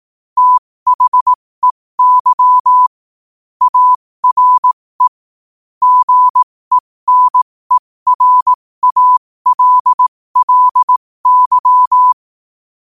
Morse code, heard as a keyed tone, about 18 wpm, spaced as for 15 wpm, 1000 Hz. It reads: THEY ARE GENERALLY
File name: qotdfiles / 15wpm